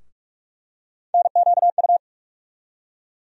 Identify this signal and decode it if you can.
Morse code, heard as a keyed tone, about 46 wpm, 700 Hz. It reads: NQU
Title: USA Call Sign Postfixes - 3 Letters 45wpm